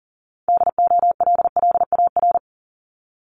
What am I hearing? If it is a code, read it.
Morse code, heard as a keyed tone, about 39 wpm, 700 Hz. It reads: DOLLAR
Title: Multi-letter Phonemes Repeated 3x 40wpm